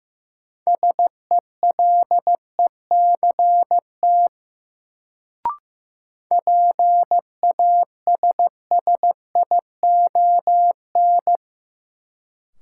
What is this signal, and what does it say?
Morse code, heard as a keyed tone, about 16 wpm, 700 Hz. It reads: SELECT PASSION
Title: Sentences from Top 1300 Words - New Word Drill - 15wpm